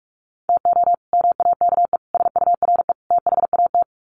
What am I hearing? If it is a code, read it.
Morse code, heard as a keyed tone, about 44 wpm, 700 Hz. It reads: TO MAKE SURE THAT
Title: Sets of 4 Words 45wpm